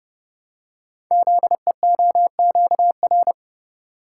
Morse code, heard as a keyed tone, about 28 wpm, 700 Hz. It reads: ZEOQR
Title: Code Groups - 5 Letters - 30wpm